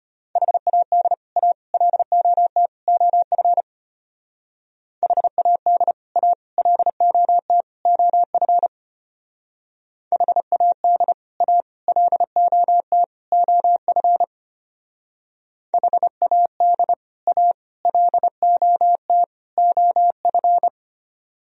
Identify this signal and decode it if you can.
Morse code, heard as a keyed tone, about 29 wpm, 700 Hz. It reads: HAD A LOT OF HAD A LOT OF HAD A LOT OF HAD A LOT OF
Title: Speed-Racing - Sets of 4 Words 25wpm